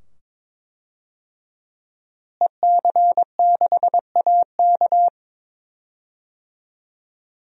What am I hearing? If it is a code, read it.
Morse code, heard as a keyed tone, about 21 wpm, 700 Hz. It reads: EC6AK
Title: DXCC-100 Call Signs - 5 Characters - Repeated 2x - 22wpm